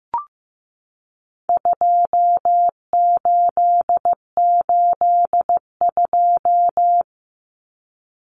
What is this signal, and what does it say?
Morse code, heard as a keyed tone, about 15 wpm, 700 Hz. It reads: 2882
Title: Common Cut Numbers - 4-Digit 15wpm